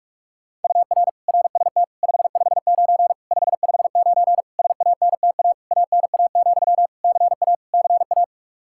Morse code, heard as a keyed tone, about 41 wpm, 700 Hz. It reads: UR RST 559 559 SANTA ANA, CA CA
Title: Practice QSO - Exchange 1 - Beginning - 45wpm